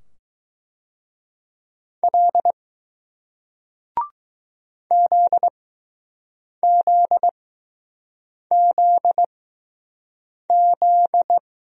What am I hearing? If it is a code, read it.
Morse code, heard as a keyed tone, about 21 wpm, spaced as for 6 wpm, 700 Hz. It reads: L ZZZZ